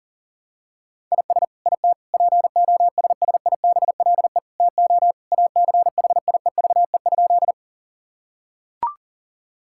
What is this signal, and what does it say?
Morse code, heard as a keyed tone, about 40 wpm, 700 Hz. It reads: IS IT POSSIBLE TO ACHIEVE?
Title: Sentences from Top 1000 Words - Review Part 6 - 40wpm